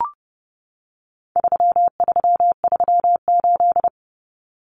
Morse code, heard as a keyed tone, about 30 wpm, 700 Hz. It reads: 3338
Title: Common Cut Numbers - 4-Digit 30wpm